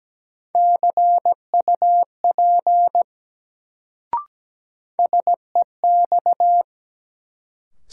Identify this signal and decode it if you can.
Morse code, heard as a keyed tone, about 17 wpm, 700 Hz. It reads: CUP SEX
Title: Sentences from Top 1600 Words - New Word Drill - 17wpm